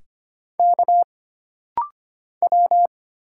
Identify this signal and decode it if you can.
Morse code, heard as a keyed tone, about 25 wpm, 700 Hz. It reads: K W
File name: Single Character (Letter:Number:Punctuation:Prosign) - Rapid-Fire - 25wpm